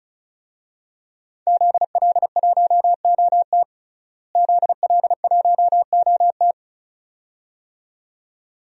Morse code, heard as a keyed tone, about 36 wpm, 700 Hz. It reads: ZL1OT ZL1OT
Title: DXCC-100 Call Signs - 5 Characters - Repeated 2x - 35wpm